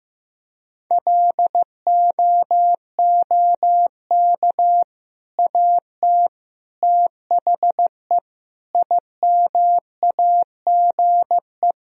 Morse code, heard as a keyed tone, about 15 wpm, 700 Hz. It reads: LOOK AT THE IMAGE
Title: Sentences from Top 1000 Words - Review Part 4 - 15wpm